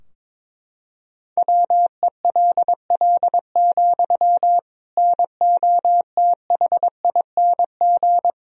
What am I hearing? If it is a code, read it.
Morse code, heard as a keyed tone, about 22 wpm, 700 Hz. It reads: WELL, NOTHING
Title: Sentences from Top 1000 Words - Review Part 3 - 22wpm